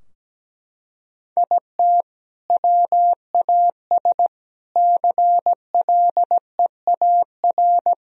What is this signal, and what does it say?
Morse code, heard as a keyed tone, about 17 wpm, 700 Hz. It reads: IT WAS CLEAR
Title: Sentences from Top 1000 Words - Review Part 4 - 17wpm